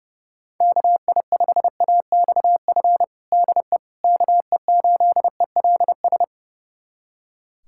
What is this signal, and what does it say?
Morse code, heard as a keyed tone, about 30 wpm, 700 Hz. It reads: KI5AXF DE KE8ELS
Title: USA Callsign DE Callsign - 6 Characters - 30wpm